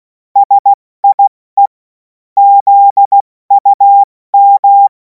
Morse code, heard as a keyed tone, about 16 wpm, 800 Hz. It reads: SIE ZUM
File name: Morsekurs / Witze / Witz 31. März 2026 70 F80